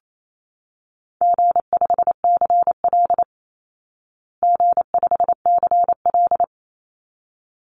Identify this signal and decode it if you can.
Morse code, heard as a keyed tone, about 28 wpm, 700 Hz. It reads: G5CL G5CL
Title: Easy Call Signs Repeated 2x - 28wpm